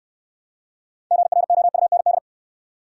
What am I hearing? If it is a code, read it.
Morse code, heard as a keyed tone, about 68 wpm, 700 Hz. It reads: GROUND